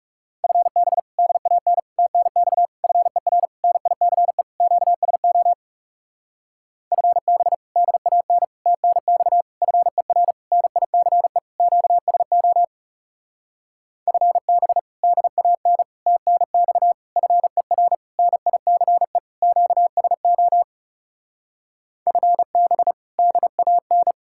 Morse code, heard as a keyed tone, about 40 wpm, 700 Hz. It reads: FB DAN TNX FER NICE QSO FB DAN TNX FER NICE QSO FB DAN TNX FER NICE QSO FB DAN